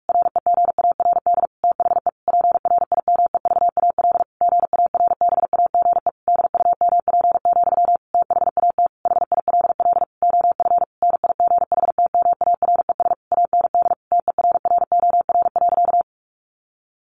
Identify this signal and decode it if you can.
Morse code, heard as a keyed tone, about 44 wpm, 700 Hz. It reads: REGARD THE PRIMEVAL GARBAGE DUMP, THAT HILL OF NIGHTMARES AND TERROR.
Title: A Sound Of Thunder 45wpm